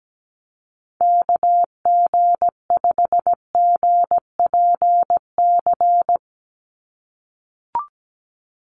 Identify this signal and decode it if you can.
Morse code, heard as a keyed tone, about 17 wpm, 700 Hz. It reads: KG5GPC
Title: USA Call Signs 17wpm